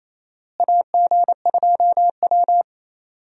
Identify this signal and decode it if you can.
Morse code, heard as a keyed tone, about 28 wpm, 700 Hz. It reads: AG2W
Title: USA Call Signs - 4 Characters - Repeated 3x - 28wpm